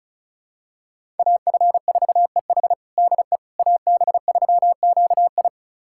Morse code, heard as a keyed tone, about 36 wpm, 700 Hz. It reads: AF4EH DE AB3QI
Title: USA Callsign DE Callsign - 5 Characters - 35wpm